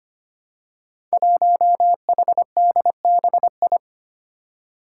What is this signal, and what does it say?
Morse code, heard as a keyed tone, about 25 wpm, 700 Hz. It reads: 1HDBI
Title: Five Challenging Letter-Number Combinations 25wpm